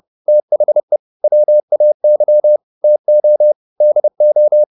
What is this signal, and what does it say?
Morse code, heard as a keyed tone, about 30 wpm, 600 Hz. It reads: THE WAY TO DO